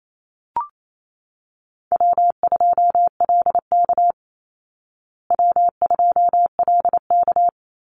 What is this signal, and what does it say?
Morse code, heard as a keyed tone, about 28 wpm, 700 Hz. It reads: W2LK W2LK
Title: Easy Call Signs Repeated 2x - 28wpm